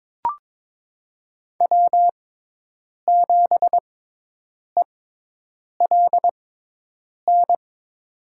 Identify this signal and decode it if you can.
Morse code, heard as a keyed tone, about 22 wpm, spaced as for 8 wpm, 700 Hz. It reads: W7ELN
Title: USA Call Signs - 5 Characters - 6x Character Spacing - 22wpm